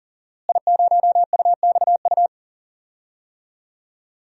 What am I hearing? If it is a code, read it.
Morse code, heard as a keyed tone, about 40 wpm, 700 Hz. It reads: I0UXU
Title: DXCC-100 Call Signs - 5 Characters - Repeated 2x - 40wpm